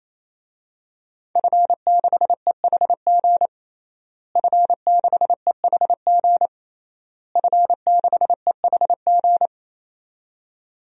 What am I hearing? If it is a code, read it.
Morse code, heard as a keyed tone, about 28 wpm, 700 Hz. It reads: F6EHG F6EHG F6EHG
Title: DXCC-100 Call Signs - 5 Characters - Repeated 3x - 28wpm